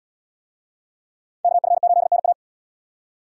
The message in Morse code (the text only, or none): KF8NA